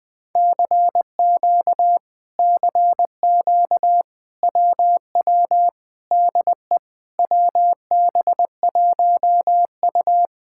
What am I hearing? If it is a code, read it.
Morse code, heard as a keyed tone, about 20 wpm, 700 Hz. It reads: CQ CQ WW DE WB1U
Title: Speed-Racing Common QSO Phrases - 17wpm